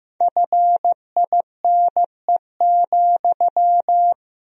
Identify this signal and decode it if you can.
Morse code, heard as a keyed tone, about 16 wpm, 700 Hz. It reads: FINE,
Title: 4x Word-Spacing - Sentences from Top 500 Words - 15wpm